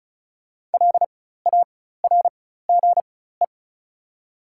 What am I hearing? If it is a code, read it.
Morse code, heard as a keyed tone, about 36 wpm, spaced as for 16 wpm, 700 Hz. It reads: LARGE
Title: 4x Character Spacing - Top 200 Words - 35wpm